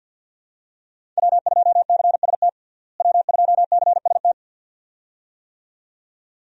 Morse code, heard as a keyed tone, about 49 wpm, 700 Hz. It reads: W2CST W2CST
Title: USA Call Signs - 5 Characters - Repeated 2x - 50wpm